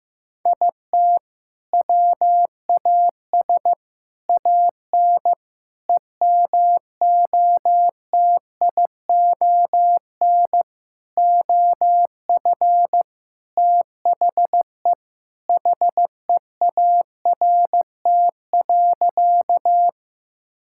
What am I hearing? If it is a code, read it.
Morse code, heard as a keyed tone, about 15 wpm, 700 Hz. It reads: IT WAS AN EMOTION OF THE HEART.
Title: Sentences from Top 1600 Words 15wpm